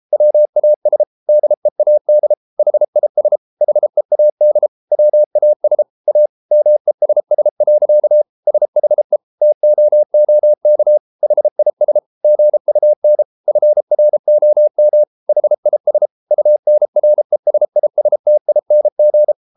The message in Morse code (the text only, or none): WAS DEAD HIS HEAD WAS A MESS. SHE TOOK HIS GUN FROM HIS UNRESISTING